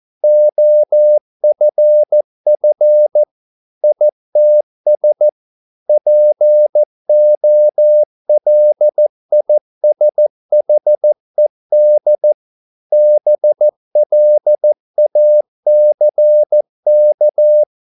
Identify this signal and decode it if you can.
Morse code, heard as a keyed tone, about 14 wpm, 600 Hz. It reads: OFF ITS POLISHED BLACK